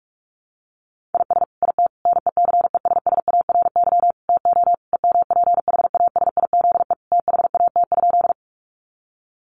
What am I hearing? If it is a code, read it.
Morse code, heard as a keyed tone, about 44 wpm, 700 Hz. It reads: IS IT NECESSARY TO EMPHASIZE THAT?
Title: Sentences from Top 1500 Words 45wpm